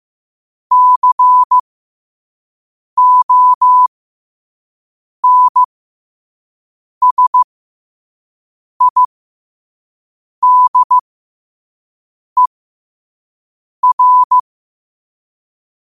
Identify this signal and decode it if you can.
Morse code, heard as a keyed tone, about 15 wpm, spaced as for 5 wpm, 1000 Hz. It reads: CONSIDER
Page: qotd-at-5-wpm